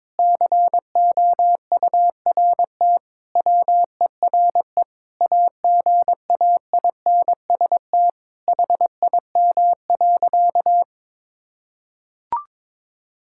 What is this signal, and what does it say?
Morse code, heard as a keyed tone, about 22 wpm, 700 Hz. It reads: COURT WERE AGAINST HIM.
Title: Sentences from Top 1600 Words 22wpm